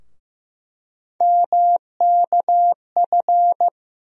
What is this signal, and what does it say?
Morse code, heard as a keyed tone, about 15 wpm, 700 Hz. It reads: MKF
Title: USA Call Sign Postfixes - 3 Letters 15wpm